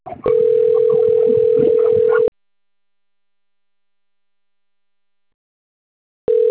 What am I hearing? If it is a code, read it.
Ringback tone (North America), 440 + 480 Hz.